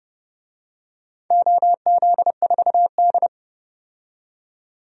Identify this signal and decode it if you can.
Morse code, heard as a keyed tone, about 32 wpm, 700 Hz. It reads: OZ4D
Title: DXCC-100 Call Signs - 4 Characters - 30wpm